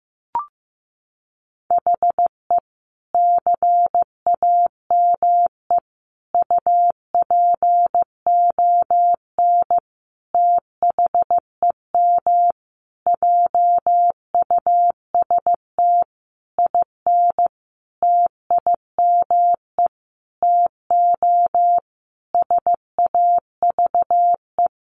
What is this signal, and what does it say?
Morse code, heard as a keyed tone, about 14 wpm, 700 Hz. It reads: HE CAME UPON THEM JUST IN TIME TO SAVE
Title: Sentences from Top 1100 Words 15wpm